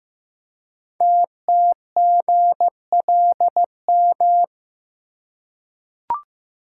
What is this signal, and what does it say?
Morse code, heard as a keyed tone, about 15 wpm, 700 Hz. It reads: TTGLM
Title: Code Groups - 5 Letters - 15wpm